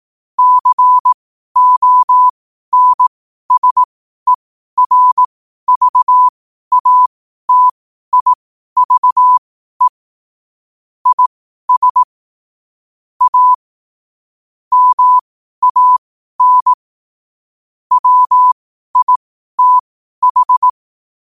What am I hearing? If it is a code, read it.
Morse code, heard as a keyed tone, about 18 wpm, spaced as for 12 wpm, 1000 Hz. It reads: CONSERVATIVE IS A MAN WITH
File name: qotdfiles / 12wpm